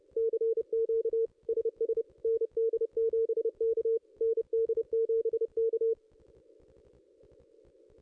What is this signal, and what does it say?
Morse code, heard as a keyed tone, about 29 wpm, 440 Hz. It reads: CQ SS ND7K ND7K
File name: contests / ss-cw-k5tr-op-n5rz -2200